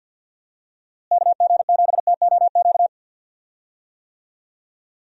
Morse code, heard as a keyed tone, about 51 wpm, 700 Hz. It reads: KG7TOQ
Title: USA Call Signs - 6 Characters - Repeated 3x - 50wpm